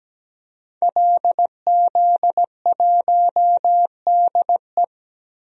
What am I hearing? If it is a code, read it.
Morse code, heard as a keyed tone, about 17 wpm, 700 Hz. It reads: LZ1DE
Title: DXCC-100 Call Signs - 5 Characters - 17wpm